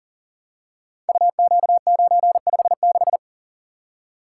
Morse code, heard as a keyed tone, about 42 wpm, 700 Hz. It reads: UQ956
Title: Code Groups - 5 Letter:Numbers - 40wpm